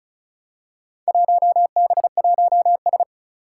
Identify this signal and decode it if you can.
Morse code, heard as a keyed tone, about 36 wpm, 700 Hz. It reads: 1B1S
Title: Four Challenging Letter-Number Combinations 35wpm